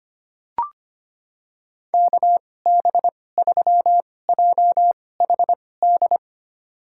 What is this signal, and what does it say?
Morse code, heard as a keyed tone, about 25 wpm, spaced as for 18 wpm, 700 Hz. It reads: KB3JHD
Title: USA Call Signs - 6 Characters - 2x Character Spacing - 25wpm